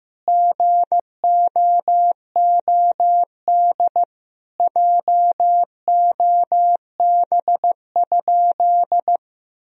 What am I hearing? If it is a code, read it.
Morse code, heard as a keyed tone, about 15 wpm, 700 Hz. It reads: GOOD JOB?
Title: Sentences from Top 1000 Words - Review Part 3 - 15wpm